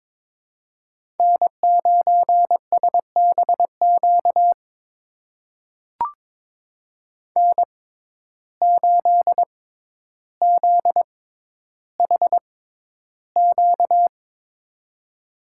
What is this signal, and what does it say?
Morse code, heard as a keyed tone, about 22 wpm, spaced as for 11 wpm, 700 Hz. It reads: N9SBQ N8ZHQ
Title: USA Call Signs - 5 Characters - 6x Character Spacing - 22wpm